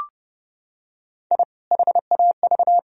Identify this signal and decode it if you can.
Morse code, heard as a keyed tone, about 30 wpm, 700 Hz. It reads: I HAV
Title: Sentences from Top 1000 Words - Review Part 1 - 30wpm